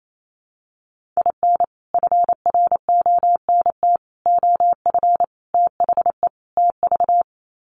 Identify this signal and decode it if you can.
Morse code, heard as a keyed tone, about 28 wpm, 700 Hz. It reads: IN FRONT OF THE TV
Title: Sets of 5 Words Encore2 28wpm